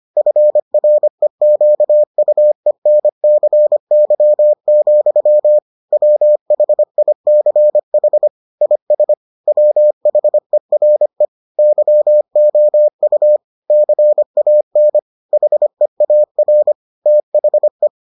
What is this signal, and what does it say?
Morse code, heard as a keyed tone, about 25 wpm, 600 Hz. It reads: FREQUENCY, WHICH IS WHERE YOU CAN HEAR THE